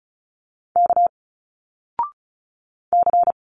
Morse code, heard as a keyed tone, about 36 wpm, 700 Hz. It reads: K C
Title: Single Letters 35wpm